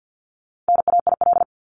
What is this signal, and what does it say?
Morse code, heard as a keyed tone, about 50 wpm, 700 Hz. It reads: NAIL
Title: Top 1501-1750 Words 50wpm